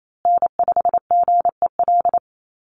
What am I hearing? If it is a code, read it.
Morse code, heard as a keyed tone, about 28 wpm, 700 Hz. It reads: N5GEL
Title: USA Call Signs Repeated 2x 28wpm